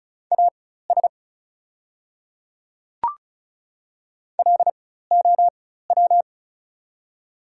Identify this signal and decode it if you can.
Morse code, heard as a keyed tone, about 36 wpm, spaced as for 16 wpm, 700 Hz. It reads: AS LOW
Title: 4x Character Spacing - Top 200 Words - 35wpm